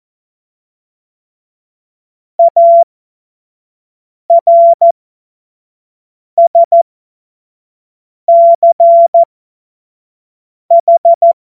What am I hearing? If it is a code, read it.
Morse code, heard as a keyed tone, about 14 wpm, spaced as for 5 wpm, 680 Hz. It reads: ARSCH
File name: Morsekurs / Witze / Witz 16. Februar 2021, 25 F70